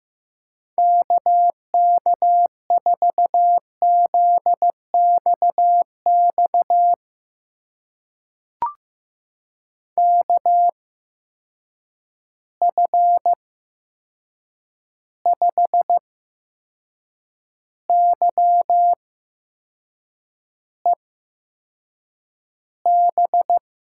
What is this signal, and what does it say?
Morse code, heard as a keyed tone, about 15 wpm, spaced as for 6 wpm, 700 Hz. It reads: KK4ZXX KF5YEB
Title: USA Call Signs - 6 Characters - 8x Character Spacing - 15wpm